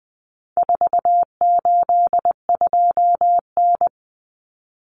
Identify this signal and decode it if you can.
Morse code, heard as a keyed tone, about 20 wpm, 700 Hz. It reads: 482N
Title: Common Cut Numbers - 4-Digit 20wpm